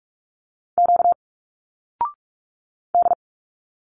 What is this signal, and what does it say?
Morse code, heard as a keyed tone, about 44 wpm, 700 Hz. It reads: Q D
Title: Single Letters 45wpm